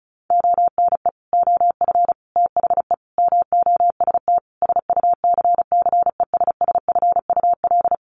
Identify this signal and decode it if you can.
Morse code, heard as a keyed tone, about 34 wpm, 700 Hz. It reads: ONE OF THE MOST SUCCESSFUL
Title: Sets of 5 Words Encore2 35wpm